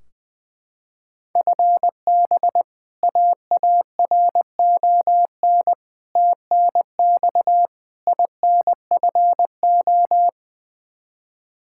Morse code, heard as a keyed tone, about 20 wpm, 700 Hz. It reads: FB AARON TNX INFO
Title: K1USN SST QSO Practice 20wpm